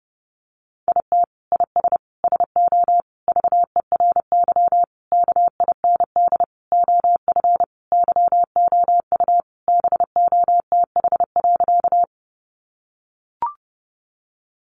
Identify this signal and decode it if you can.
Morse code, heard as a keyed tone, about 30 wpm, 700 Hz. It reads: IT IS SO VERY KIND OF YOU BOTH.
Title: Sentences from Top 300 Words 30wpm